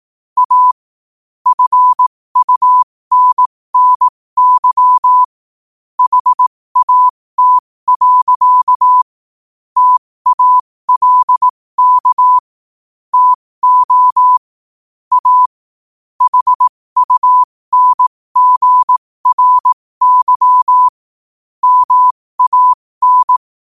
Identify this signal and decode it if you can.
Morse code, heard as a keyed tone, about 18 wpm, spaced as for 15 wpm, 1000 Hz. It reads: A FUNNY HAT. TALK TO A HUNGRY MAN
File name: qotdfiles / 15wpm